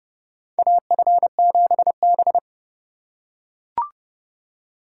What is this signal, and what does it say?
Morse code, heard as a keyed tone, about 30 wpm, 700 Hz. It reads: AF7B
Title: USA Call Signs - 4 Characters - 3x Character Spacing - 30wpm